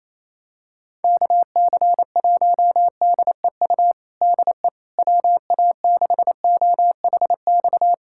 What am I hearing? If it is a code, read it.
Morse code, heard as a keyed tone, about 27 wpm, 700 Hz. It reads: KC1DEU DE WA6OHX
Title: USA Callsign DE Callsign - 6 Characters - 28wpm